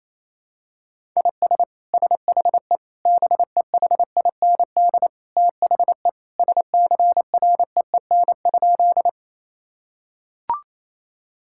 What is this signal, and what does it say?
Morse code, heard as a keyed tone, about 28 wpm, 700 Hz. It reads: IS SHE BEHIND THE SCREEN?
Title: Sentences from Top 1000 Words - Review Part 6 - 28wpm